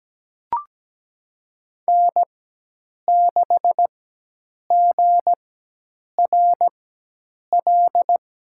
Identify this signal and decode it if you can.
Morse code, heard as a keyed tone, about 17 wpm, spaced as for 8 wpm, 700 Hz. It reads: N6GRL
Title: USA Call Signs - 4x Character Spacing - 17wpm